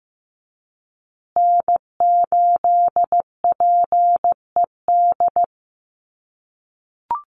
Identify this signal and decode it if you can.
Morse code, heard as a keyed tone, about 15 wpm, 700 Hz. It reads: N8PED
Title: USA Call Signs Repeated 2x 15wpm